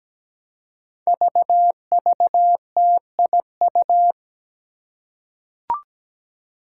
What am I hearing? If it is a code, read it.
Morse code, heard as a keyed tone, about 17 wpm, 700 Hz. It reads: VVTIU
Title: Five Challenging Letter-Number Combinations 17wpm